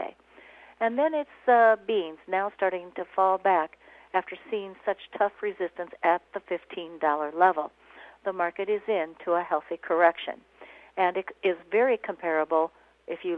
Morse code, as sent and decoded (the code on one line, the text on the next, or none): none